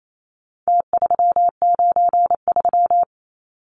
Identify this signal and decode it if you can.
Morse code, heard as a keyed tone, about 28 wpm, 700 Hz. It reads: T393
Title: Numbers - Contest Style - 4-Digit 28wpm